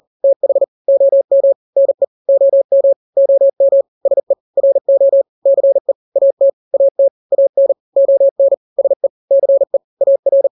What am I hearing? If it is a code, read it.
Morse code, heard as a keyed tone, about 40 wpm, 550 Hz. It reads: TH OM NE OM OM SE RO CE AT AT AN ON SE CE AR
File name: n-grams / 2-grams 40 35 WPM